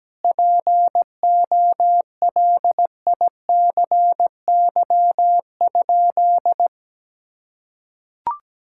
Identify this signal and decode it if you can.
Morse code, heard as a keyed tone, about 17 wpm, 700 Hz. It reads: POLICY?
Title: Sentences from Top 1000 Words - Review Part 5 - 17wpm